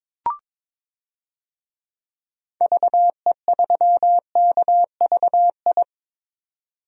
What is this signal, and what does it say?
Morse code, heard as a keyed tone, about 22 wpm, 700 Hz. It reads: VE3KVI
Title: DXCC-100 Call Signs - 6 Characters - 22wpm